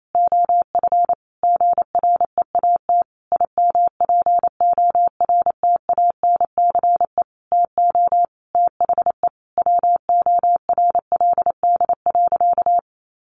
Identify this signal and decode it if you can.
Morse code, heard as a keyed tone, about 28 wpm, 700 Hz. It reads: OF GREAT IMPORTANCE TO THE WORLD.
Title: Sentences from Top 1200 Words 28wpm